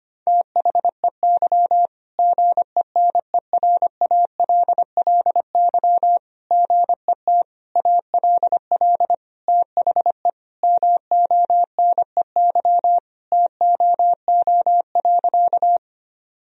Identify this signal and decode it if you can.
Morse code, heard as a keyed tone, about 25 wpm, 700 Hz. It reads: THEY GENERALLY GET ALL THE MONEY TOO.
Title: Sentences from Top 1000 Words - Review Part 4 - 25wpm